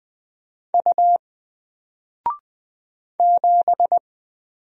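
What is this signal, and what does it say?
Morse code, heard as a keyed tone, about 20 wpm, 700 Hz. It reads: U 7
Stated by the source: Single letter and number Morse drill at 20 wpm, from N4LCD.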